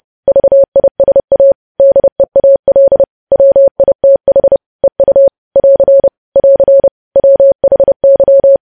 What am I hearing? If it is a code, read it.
Morse code, heard as a keyed tone, about 30 wpm, 550 Hz. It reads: VISA DEAL WITH EU <AR> <AR> WHY
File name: podcast / News headlines-1x-30x30